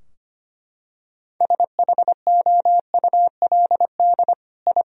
Morse code, heard as a keyed tone, about 25 wpm, 700 Hz. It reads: SHOULD I
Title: Sentences from Top 1000 Words - Review Part 2 - 25wpm